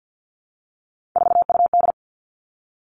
Morse code, heard as a keyed tone, about 50 wpm, 700 Hz. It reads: 4UD
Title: All Cut Numbers - 3-Digit 50wpm